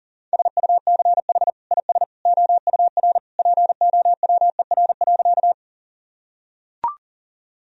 Morse code, heard as a keyed tone, about 40 wpm, 700 Hz. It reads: SUCH IS OUR POWER.
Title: Sentences from Top 1000 Words - Review Part 2 - 40wpm